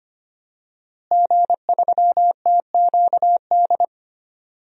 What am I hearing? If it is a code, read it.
Morse code, heard as a keyed tone, about 25 wpm, 700 Hz. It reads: G3TQD
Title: DXCC-100 Call Signs - 5 Characters - Repeated 3x - 25wpm